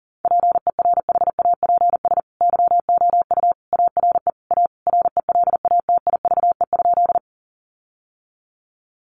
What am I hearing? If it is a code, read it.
Morse code, heard as a keyed tone, about 40 wpm, 700 Hz. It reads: PERHAPS YOU ARE A RELATIVE?
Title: Sentences from Top 1400 Words 40wpm